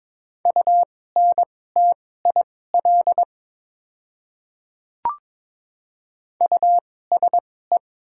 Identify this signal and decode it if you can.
Morse code, heard as a keyed tone, about 22 wpm, spaced as for 16 wpm, 700 Hz. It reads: UNTIL USE
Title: 2x Character Spacing - Top 300 Words - 22wpm